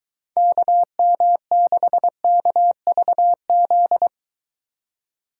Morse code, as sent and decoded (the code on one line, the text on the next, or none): -.- -- -.... -.- ...- --..
KM6KVZ